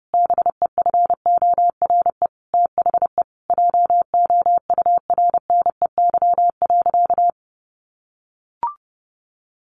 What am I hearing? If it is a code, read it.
Morse code, heard as a keyed tone, about 30 wpm, 700 Hz. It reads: BEFORE THE JOURNEY.
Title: A Sound Of Thunder 30wpm